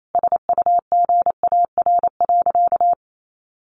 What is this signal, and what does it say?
Morse code, heard as a keyed tone, about 28 wpm, 700 Hz. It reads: SUGAR.
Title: Sentences from Top 1200 Words 28wpm